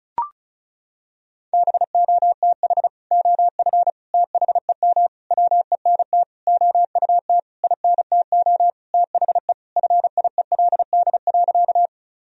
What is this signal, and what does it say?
Morse code, heard as a keyed tone, about 34 wpm, 700 Hz. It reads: BOTH OF THEM WENT OUT INTO THE FIELD.
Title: Sentences from Top 1000 Words - Review Part 2 - 35wpm